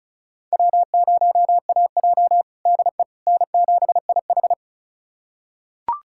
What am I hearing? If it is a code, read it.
Morse code, heard as a keyed tone, about 34 wpm, 700 Hz. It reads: W0AJ DE N7IH
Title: USA Callsign DE Callsign - 4 Characters - 35wpm